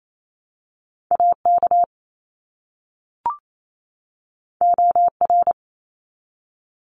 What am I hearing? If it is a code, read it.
Morse code, heard as a keyed tone, about 28 wpm, 700 Hz. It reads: AK OR
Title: ARRL Sections 28wpm